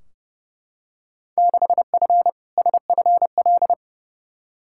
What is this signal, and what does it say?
Morse code, heard as a keyed tone, about 30 wpm, 700 Hz. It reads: 6F SFL
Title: Field Day Exchanges Encore - 30wpm